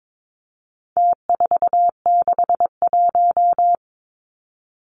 Morse code, heard as a keyed tone, about 22 wpm, 700 Hz. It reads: T461
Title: Numbers - Contest Style - 4-Digit 22wpm